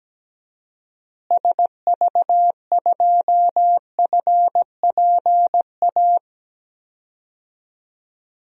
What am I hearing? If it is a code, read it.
Morse code, heard as a keyed tone, about 17 wpm, 700 Hz. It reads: SV2FPA
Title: DXCC-100 Call Signs - 6 Characters - Repeated 3x - 17wpm